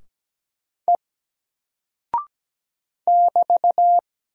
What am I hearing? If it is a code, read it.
Morse code, heard as a keyed tone, about 17 wpm, 700 Hz. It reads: E <BT>
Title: Single Character (Letter:Number:Punctuation:Prosign) 17wpm